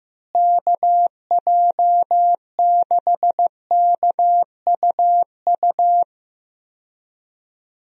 Morse code, heard as a keyed tone, about 14 wpm, 700 Hz. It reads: KJ6KUU
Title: USA Call Signs - 6 Characters - Repeated 3x - 15wpm